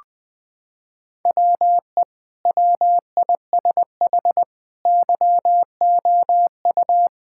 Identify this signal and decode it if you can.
Morse code, heard as a keyed tone, about 20 wpm, 700 Hz. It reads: WE WISH YOU
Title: Sentences from Top 1600 Words 20wpm